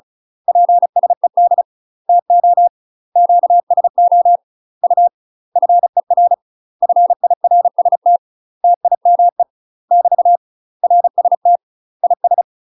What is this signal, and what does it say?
Morse code, heard as a keyed tone, about 36 wpm, 700 Hz. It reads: PSED TO QSO U FER FIRST TIME <BT> RST IS
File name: UK MASTER QSOs TRAINING / 35WPM / UK MASTER QSOs 0010